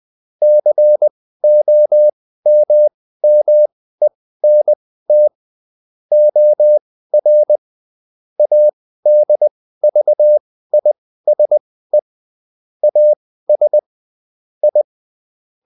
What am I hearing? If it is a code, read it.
Morse code, heard as a keyed tone, about 20 wpm, spaced as for 14 wpm, 600 Hz. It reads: COMMENT OR ADVISE AS I